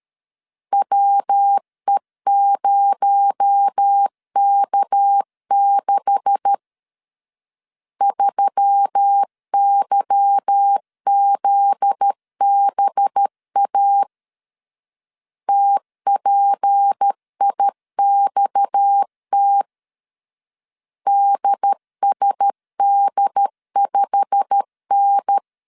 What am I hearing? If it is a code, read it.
Morse code, heard as a keyed tone, about 13 wpm, 800 Hz. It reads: WE0K6 3YZBA TPIXT DSD5N
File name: CW / DET / 1 9 1500C S C 12wpm 3811